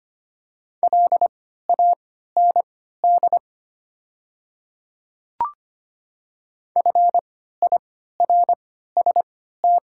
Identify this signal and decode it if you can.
Morse code, heard as a keyed tone, about 25 wpm, spaced as for 14 wpm, 700 Hz. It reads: LAND FIRST
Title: 3x Character Spacing - Top 200 Words - 25wpm